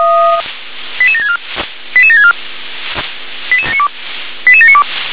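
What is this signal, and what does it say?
Five-tone selective call (ZVEI), heard as quick runs of five tones, tones of about 70 ms. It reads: 88764 88764 88792
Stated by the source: ZVEI five-tone selective call received on an SDR.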